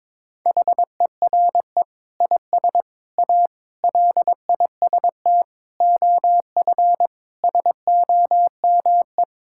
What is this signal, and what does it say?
Morse code, heard as a keyed tone, about 22 wpm, 700 Hz. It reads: HERE IS A LIST OF SOME
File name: practice / 22wpm / Sets of 8 Words - 22wpm